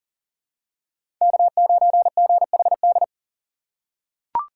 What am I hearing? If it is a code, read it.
Morse code, heard as a keyed tone, about 40 wpm, 700 Hz. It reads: K9GHD
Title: USA Call Signs - 5 Characters - 6x Character Spacing - 40wpm